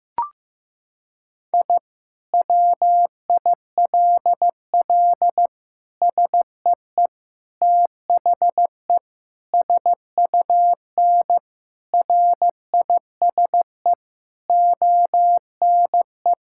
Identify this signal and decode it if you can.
Morse code, heard as a keyed tone, about 15 wpm, 700 Hz. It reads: I WILL SEE THE SUN RISE ONE
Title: Sentences from Top 1000 Words - Review Part 6 - 15wpm